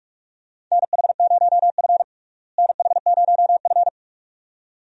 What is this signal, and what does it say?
Morse code, heard as a keyed tone, about 43 wpm, 700 Hz. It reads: NH0F NH0F
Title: USA Call Signs - 4 Characters - Repeated 2x - 45wpm